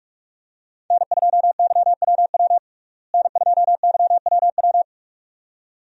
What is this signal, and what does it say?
Morse code, heard as a keyed tone, about 44 wpm, 700 Hz. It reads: N2YWW N2YWW
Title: USA Call Signs - 5 Characters - Repeated 2x - 45wpm